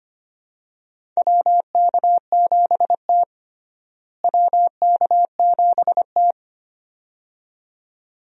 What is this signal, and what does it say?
Morse code, heard as a keyed tone, about 25 wpm, 700 Hz. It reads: WK7T WK7T
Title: USA Call Signs - 4 Characters - Repeated 2x - 25wpm